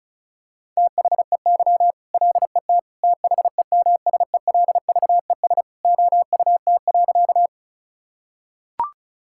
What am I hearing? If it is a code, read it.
Morse code, heard as a keyed tone, about 34 wpm, 700 Hz. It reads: THEY LET THEMSELVES OUT.
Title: Sentences from Top 1000 Words - Review Part 5 - 35wpm